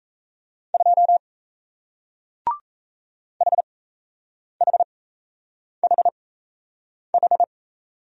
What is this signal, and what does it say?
Morse code, heard as a keyed tone, about 38 wpm, spaced as for 8 wpm, 700 Hz. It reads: 2 HHHH